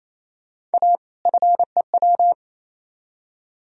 Morse code, heard as a keyed tone, about 35 wpm, 700 Hz. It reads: A FEW